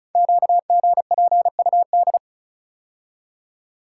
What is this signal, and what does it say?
Morse code, heard as a keyed tone, about 36 wpm, 700 Hz. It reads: QGPUD
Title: Code Groups - 5 Letters - 35wpm